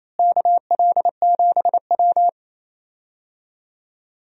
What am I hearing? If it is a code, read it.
Morse code, heard as a keyed tone, about 28 wpm, 700 Hz. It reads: KL7W
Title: DXCC-100 Call Signs - 4 Characters - Repeated 3x - 28wpm